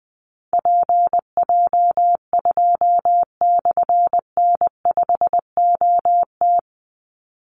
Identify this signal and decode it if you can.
Morse code, heard as a keyed tone, about 20 wpm, 700 Hz. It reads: PJ2/N5OT
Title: Call Signs - 20wpm